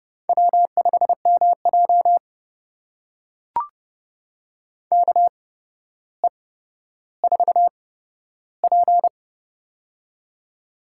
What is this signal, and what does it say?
Morse code, heard as a keyed tone, about 30 wpm, spaced as for 13 wpm, 700 Hz. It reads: W5MJ KE4P
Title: USA Call Signs - 4 Characters - 8x Character Spacing - 30wpm